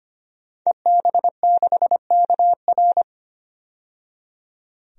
Morse code, heard as a keyed tone, about 25 wpm, 700 Hz. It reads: EB6KR
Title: DXCC-100 Call Signs - 5 Characters - 25wpm